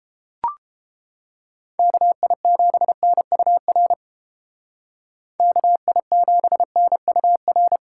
Morse code, heard as a keyed tone, about 32 wpm, 700 Hz. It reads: KI7NUR KI7NUR